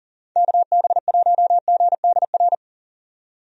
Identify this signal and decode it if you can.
Morse code, heard as a keyed tone, about 40 wpm, 700 Hz. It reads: KB1GDR
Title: USA Call Signs - 6 Characters - 40wpm